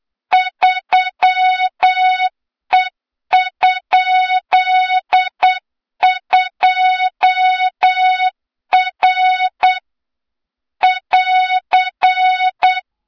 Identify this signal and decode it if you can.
Morse code, heard as a keyed tone, about 8 wpm, 3700 Hz, with noise. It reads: 3E?2R <AR>
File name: Morze-HA2MN / hang / 23 GYAKORLAT